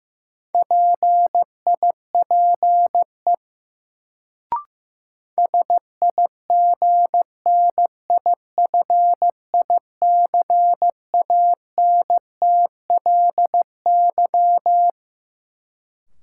Morse code, heard as a keyed tone, about 15 wpm, 700 Hz. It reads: PIPE SIGNIFICANTLY
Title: Sentences from Top 1500 Words - New Word Drill - 15wpm